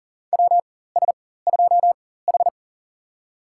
Morse code, heard as a keyed tone, about 40 wpm, spaced as for 19 wpm, 700 Hz. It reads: WS2H
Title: USA Call Signs - 4 Characters - 4x Character Spacing - 40wpm